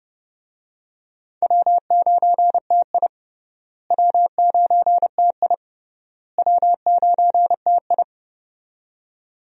Morse code, heard as a keyed tone, about 30 wpm, 700 Hz. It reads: W9TI W9TI W9TI
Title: USA Call Signs - 4 Characters - Repeated 3x - 30wpm